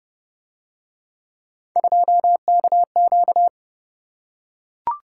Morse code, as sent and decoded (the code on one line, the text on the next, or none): ..--- -.- --.-
2KQ